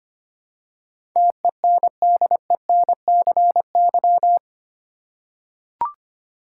Morse code, heard as a keyed tone, about 25 wpm, 700 Hz. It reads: TENDENCY
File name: Eight Letter Words - 25wpm